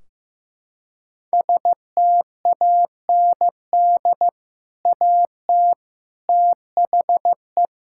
Morse code, heard as a keyed tone, about 16 wpm, 700 Hz. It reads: STAND AT THE
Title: Sentences from Top 1600 Words 15wpm